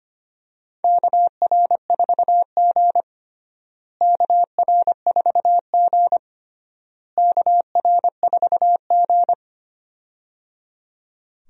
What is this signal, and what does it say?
Morse code, heard as a keyed tone, about 25 wpm, 700 Hz. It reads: KR4G KR4G KR4G
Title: USA Call Signs - 4 Characters - Repeated 3x - 25wpm